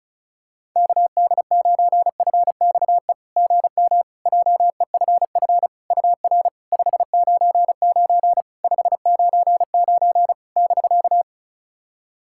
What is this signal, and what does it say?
Morse code, heard as a keyed tone, about 36 wpm, 690 Hz. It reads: KD9FXE GM JEFF UR 599 599 <BK>
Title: SOTA QSO Practice 35wpm